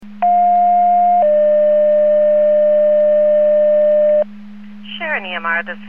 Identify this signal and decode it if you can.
Two-tone page, heard as a one-second tone, then a three-second tone, about 710 then 600 Hz, with speech. It is heard from 0.2 to 4.2 s.